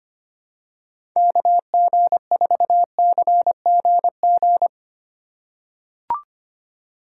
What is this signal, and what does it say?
Morse code, heard as a keyed tone, about 25 wpm, 700 Hz. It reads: KG4CGG
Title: USA Call Signs - 3x Character Spacing - 25wpm